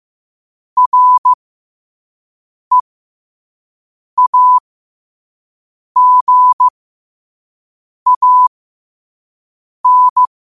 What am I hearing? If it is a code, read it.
Morse code, heard as a keyed tone, about 15 wpm, spaced as for 5 wpm, 1000 Hz. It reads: REAGAN